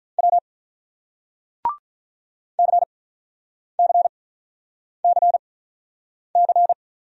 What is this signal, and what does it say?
Morse code, heard as a keyed tone, about 48 wpm, 700 Hz. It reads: W C C C C